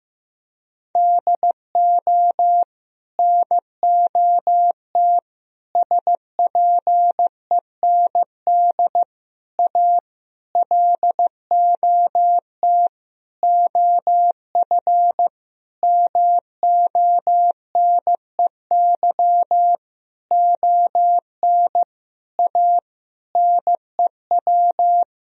Morse code, heard as a keyed tone, about 16 wpm, 700 Hz. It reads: DO NOT SPEND A LOT OF MONEY ON A NEW
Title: Sentences from Top 1000 Words - Review Part 4 - 15wpm